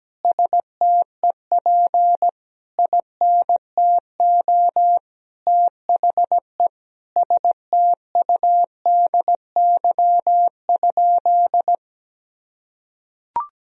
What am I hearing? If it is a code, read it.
Morse code, heard as a keyed tone, about 17 wpm, 700 Hz. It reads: STEP INTO THE STUDY?
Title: Sentences from Top 1000 Words - Review Part 2 - 17wpm